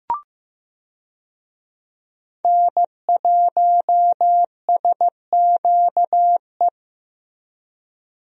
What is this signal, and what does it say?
Morse code, heard as a keyed tone, about 15 wpm, 700 Hz. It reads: N1SQE
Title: USA Call Signs - 5 Characters - Repeated 2x - 15wpm